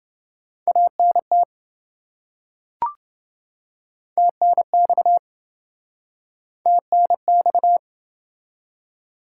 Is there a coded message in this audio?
Yes: Morse code from 0.7 to 7.8 s, heard as a keyed tone, about 26 wpm, 700 Hz. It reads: ANT TNX TNX